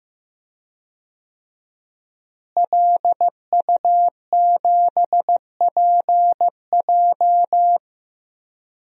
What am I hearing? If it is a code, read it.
Morse code, heard as a keyed tone, about 15 wpm, 700 Hz. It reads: LU7PJ
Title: DXCC-100 Call Signs - 5 Characters - Repeated 3x - 15wpm